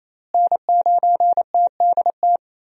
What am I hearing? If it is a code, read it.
Morse code, heard as a keyed tone, about 28 wpm, 700 Hz. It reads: N9TDT
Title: USA Call Signs - 5 Characters - Repeated 3x - 28wpm